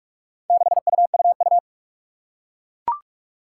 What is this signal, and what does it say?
Morse code, heard as a keyed tone, about 44 wpm, 700 Hz. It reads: 6UUU
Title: Four Challenging Letter-Number Combinations 45wpm